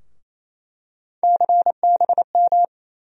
Morse code, heard as a keyed tone, about 28 wpm, 700 Hz. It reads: CBM
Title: USA Call Sign Postfixes - 3 Letters 28wpm